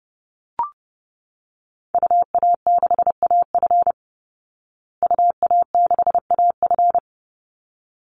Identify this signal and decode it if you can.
Morse code, heard as a keyed tone, about 30 wpm, 700 Hz. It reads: UA6AF UA6AF
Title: Call Signs Repeated 3x - 30wpm